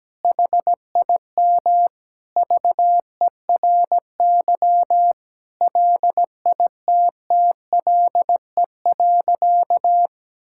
Morse code, heard as a keyed tone, about 17 wpm, 700 Hz. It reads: HIM VERY LITTLE.
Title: Sentences from Top 1000 Words - Review Part 5 - 17wpm